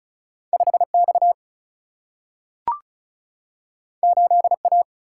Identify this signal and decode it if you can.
Morse code, heard as a keyed tone, about 36 wpm, 700 Hz. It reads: 5X 8A
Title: Single-Letter Single-Number 35wpm